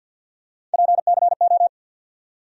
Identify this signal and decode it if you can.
Morse code, heard as a keyed tone, about 53 wpm, 700 Hz. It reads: PCO
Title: USA Call Sign Postfixes - 3 Letters 50wpm